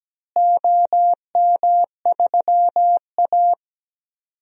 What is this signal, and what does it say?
Morse code, heard as a keyed tone, about 17 wpm, 700 Hz. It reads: OM3A
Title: DXCC-100 Call Signs - 4 Characters - Repeated 3x - 17wpm